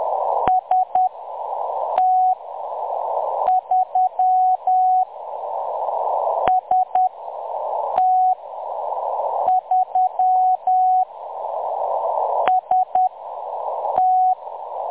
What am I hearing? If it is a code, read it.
Morse code, heard as a keyed tone, about 10 wpm, spaced as for 6 wpm, 740 Hz. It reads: ST3 ST3 ST
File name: M23-ST3- -0858utc-10184khz